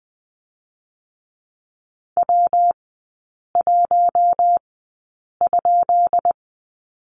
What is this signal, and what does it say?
Morse code, heard as a keyed tone, about 20 wpm, spaced as for 8 wpm, 700 Hz. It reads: W1?
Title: 218 Lesson 35e - Number 0 - Three Characters 20wpm